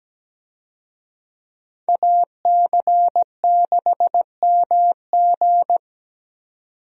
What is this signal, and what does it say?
Morse code, heard as a keyed tone, about 17 wpm, 700 Hz. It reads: AC6MG
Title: USA Call Signs - 5 Characters - 17wpm